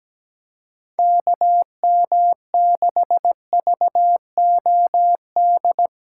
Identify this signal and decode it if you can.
Morse code, heard as a keyed tone, about 17 wpm, 700 Hz. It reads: KM6VOD
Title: USA Callsign DE Callsign 17wpm